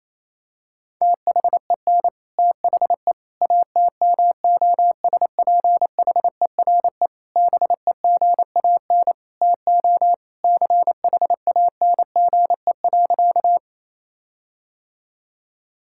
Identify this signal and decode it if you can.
Morse code, heard as a keyed tone, about 28 wpm, 700 Hz. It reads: THEN THE ATMOSPHERE BEGAN TO CHANGE.
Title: Sentences from Top 1600 Words 28wpm